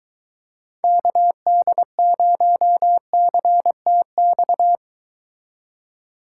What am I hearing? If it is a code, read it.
Morse code, heard as a keyed tone, about 23 wpm, 700 Hz. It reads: KD0CTX